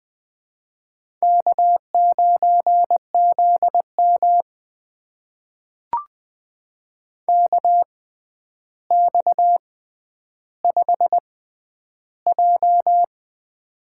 Morse code, heard as a keyed tone, about 21 wpm, spaced as for 10 wpm, 700 Hz. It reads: K9ZM KX5J
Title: USA Call Signs - 4 Characters - 6x Character Spacing - 20wpm